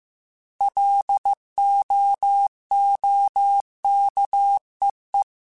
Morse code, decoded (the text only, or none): LOOKEE